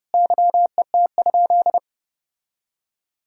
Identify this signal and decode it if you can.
Morse code, heard as a keyed tone, about 30 wpm, 700 Hz. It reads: YET?
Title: Sentences from Top 1000 Words - Review Part 5 - 30wpm